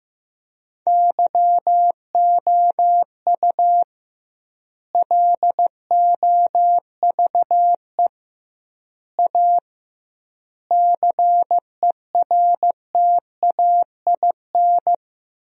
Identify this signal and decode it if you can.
Morse code, heard as a keyed tone, about 15 wpm, 700 Hz. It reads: YOU LOVE A CERTAIN
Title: 2x Word-Spacing - Sentences from Top 500 Words - 15wpm